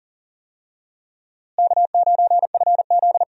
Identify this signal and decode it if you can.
Morse code, heard as a keyed tone, about 40 wpm, 700 Hz. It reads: K9FZ
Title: USA Call Signs - 4 Characters - Repeated 2x - 40wpm